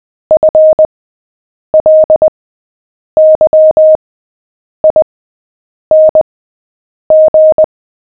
Morse code, heard as a keyed tone, about 20 wpm, spaced as for 8 wpm, 620 Hz. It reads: FLYING
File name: cw / fists / Short-Story-7-8wpm